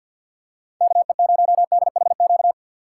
Morse code, heard as a keyed tone, about 53 wpm, 700 Hz. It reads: KE0DHQ
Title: USA Call Signs - 6 Characters - 4x Character Spacing - 50wpm